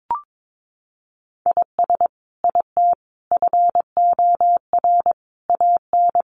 Morse code, heard as a keyed tone, about 22 wpm, 700 Hz. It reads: IS IT FOR AN
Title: Sentences from Top 1200 Words 22wpm